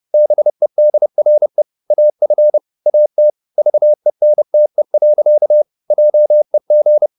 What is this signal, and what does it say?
Morse code, heard as a keyed tone, about 30 wpm, 600 Hz. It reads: BEDRE AF AT VENTE. JEG
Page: Ferie 30 wpm